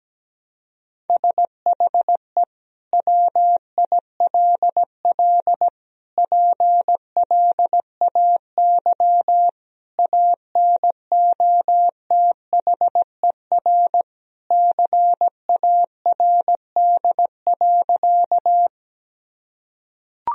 Morse code, heard as a keyed tone, about 17 wpm, 700 Hz. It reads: SHE WILL PLAY ANOTHER CARD.
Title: Sentences from Top 1000 Words - Review Part 4 - 17wpm